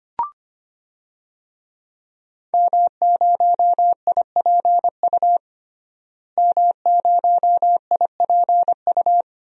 Morse code, heard as a keyed tone, about 25 wpm, 700 Hz. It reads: M0IPU M0IPU
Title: DXCC-100 Call Signs - 5 Characters - Repeated 2x - 25wpm